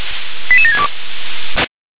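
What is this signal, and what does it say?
Five-tone selective call (ZVEI), heard as quick runs of five tones, tones of about 70 ms. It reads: 88762
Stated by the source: ZVEI five-tone call 88740 received by SDR.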